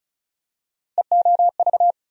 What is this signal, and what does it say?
Morse code, heard as a keyed tone, about 34 wpm, 700 Hz. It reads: EOV
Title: USA Call Sign Postfixes - 3 Letters 35wpm